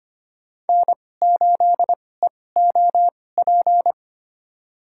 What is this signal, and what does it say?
Morse code, heard as a keyed tone, about 25 wpm, spaced as for 18 wpm, 700 Hz. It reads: N8EOP
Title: USA Call Signs - 5 Characters - 2x Character Spacing - 25wpm